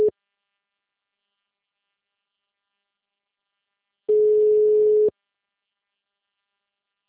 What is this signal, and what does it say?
Ringback tone (Europe, 425 Hz).